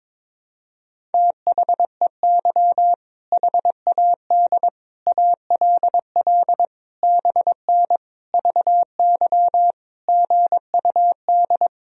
Morse code, heard as a keyed tone, about 22 wpm, 700 Hz. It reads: THEY HAD ALL BN VY GUD
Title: Sentences from Top 200 Words and Abbreviations 22wpm